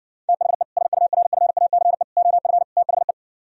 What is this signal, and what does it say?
Morse code, heard as a keyed tone, about 59 wpm, 700 Hz. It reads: THE SURFACE OF THE